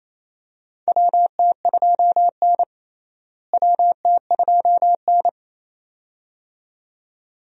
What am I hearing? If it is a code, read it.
Morse code, heard as a keyed tone, about 28 wpm, 700 Hz. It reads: WT2N WT2N
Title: USA Call Signs - 4 Characters - Repeated 2x - 28wpm